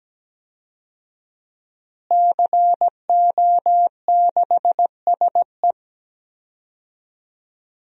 Morse code, heard as a keyed tone, about 17 wpm, 700 Hz. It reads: CO6SE
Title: DXCC-100 Call Signs - 5 Characters - Repeated 2x - 17wpm